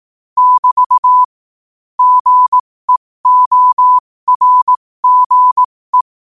Morse code, heard as a keyed tone, about 18 wpm, spaced as for 15 wpm, 1000 Hz. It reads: <BT> GEORGE
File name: qotdfiles / 15wpm